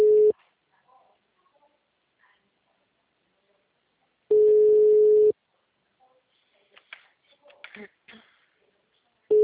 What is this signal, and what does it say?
Ringback tone (Europe, 425 Hz).